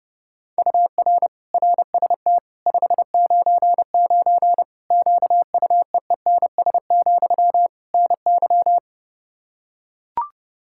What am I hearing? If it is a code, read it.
Morse code, heard as a keyed tone, about 30 wpm, 700 Hz. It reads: UR RST 599 QUEENS, NY
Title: Practice QSO - Exchange 1 - Beginning - 30wpm